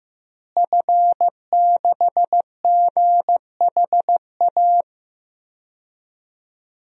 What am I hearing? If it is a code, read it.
Morse code, heard as a keyed tone, about 15 wpm, 700 Hz. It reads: F6GHA
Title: DXCC-100 Call Signs - 5 Characters - Repeated 2x - 15wpm